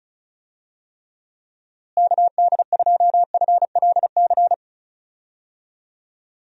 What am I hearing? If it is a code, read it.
Morse code, heard as a keyed tone, about 36 wpm, 700 Hz. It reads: KD2FLC
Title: USA Call Signs - 6 Characters - Repeated 2x - 35wpm